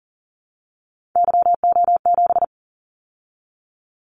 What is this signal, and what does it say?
Morse code, heard as a keyed tone, about 40 wpm, 700 Hz. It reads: YO7
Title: Callsign Prefixes for Top 25 DXCC - 40wpm